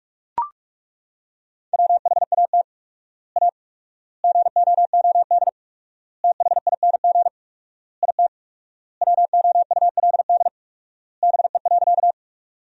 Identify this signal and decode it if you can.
Morse code, heard as a keyed tone, about 42 wpm, 700 Hz. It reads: WHAT A GOOD THING IT WOULD BE.
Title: 4x Word-Spacing - Sentences from Top 200 Words - 45wpm